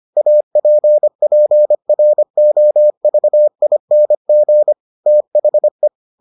Morse code, heard as a keyed tone, about 25 wpm, 600 Hz. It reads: APPROVING THE